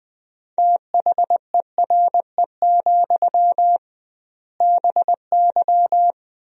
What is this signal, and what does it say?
Morse code, heard as a keyed tone, about 20 wpm, 700 Hz. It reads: THERE, BY
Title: 2x Word-Spacing - Sentences from Top 500 Words - 20wpm